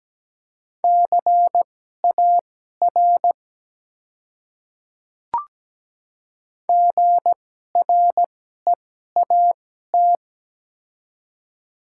Morse code, heard as a keyed tone, about 17 wpm, spaced as for 12 wpm, 700 Hz. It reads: CAR GREAT
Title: 2x Character Spacing - Top 300 Words - 17wpm